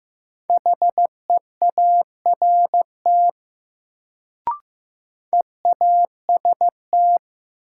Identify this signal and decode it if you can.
Morse code, heard as a keyed tone, about 15 wpm, 700 Hz. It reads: HEART EAST
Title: Sets of 3 Words - New Word Drill - 15wpm